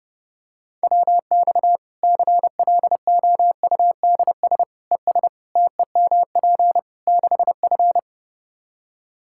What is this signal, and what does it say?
Morse code, heard as a keyed tone, about 31 wpm, 700 Hz. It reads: WX CLOUDS ES TEMP 6F
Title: Practice QSO - Exchange 2 - Info - 30wpm